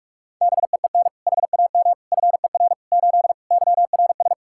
Morse code, heard as a keyed tone, about 45 wpm, 700 Hz. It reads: BEEN HAM FER 8 YRS
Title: Practice QSO - Exchange 3 - More Info - 45wpm